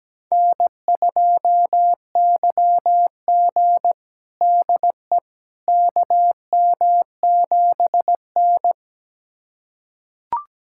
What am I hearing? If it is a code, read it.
Morse code, heard as a keyed tone, about 17 wpm, 700 Hz. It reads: N2YG DE KM7N
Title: USA Callsign DE Callsign - 4 Characters - 17wpm